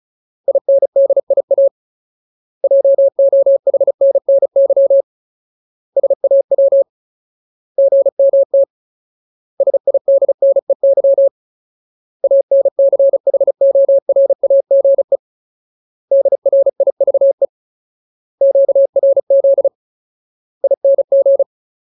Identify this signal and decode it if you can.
Morse code, heard as a keyed tone, about 34 wpm, 550 Hz. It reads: INDIA JOHNNY SAW GMT SIDNEY ANCHORAGE DRIVE QRZ ING
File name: Ultimate Word List / 35WPM Ultimate